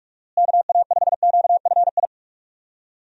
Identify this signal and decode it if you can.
Morse code, heard as a keyed tone, about 43 wpm, 700 Hz. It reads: KA5QFI
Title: USA Call Signs - 6 Characters - Repeated 3x - 45wpm